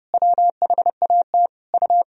Morse code, heard as a keyed tone, about 32 wpm, 700 Hz. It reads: WHAT U
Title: Sentences from Top 200 Words and Abbreviations 30wpm